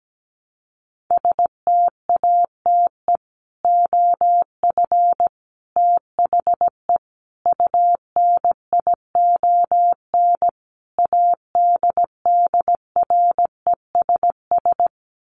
Morse code, heard as a keyed tone, about 17 wpm, 700 Hz. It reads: STATE OF THE UNION ADDRESS
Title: Sets of 5 Words Encore 17wpm